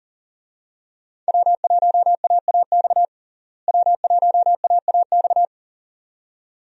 Morse code, heard as a keyed tone, about 37 wpm, 700 Hz. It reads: W1AAX W1AAX
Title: USA Call Signs - 5 Characters - Repeated 2x - 40wpm